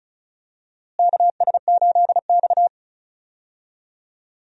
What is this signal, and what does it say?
Morse code, heard as a keyed tone, about 36 wpm, 700 Hz. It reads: KS8X
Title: USA Call Signs - 4 Characters - Repeated 3x - 35wpm